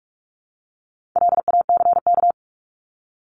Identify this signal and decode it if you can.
Morse code, heard as a keyed tone, about 44 wpm, 700 Hz. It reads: LACK
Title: Top 1001-1250 Words 45wpm